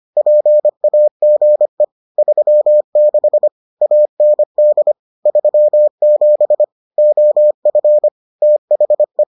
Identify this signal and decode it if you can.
Morse code, heard as a keyed tone, about 25 wpm, 600 Hz. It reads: PAGE 36 AND 37 OF THE